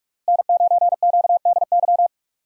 Morse code, heard as a keyed tone, about 46 wpm, 700 Hz. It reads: N9QDY
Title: USA Call Signs - 5 Characters - Repeated 3x - 45wpm